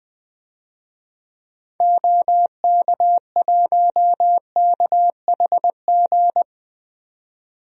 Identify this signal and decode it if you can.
Morse code, heard as a keyed tone, about 21 wpm, 700 Hz. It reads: OK1KHG
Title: DXCC-100 Call Signs - 6 Characters - 20wpm